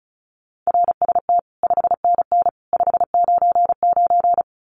Morse code, heard as a keyed tone, about 34 wpm, 700 Hz. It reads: RST 5NN 599
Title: Common QSO Phrases 35wpm